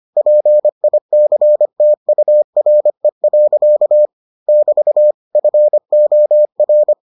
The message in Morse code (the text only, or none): PICTURE. <BT> FOR